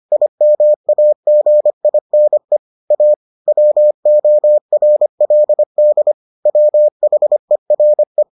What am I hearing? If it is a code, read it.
Morse code, heard as a keyed tone, about 25 wpm, 600 Hz. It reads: IMAGINE A WORLD WHERE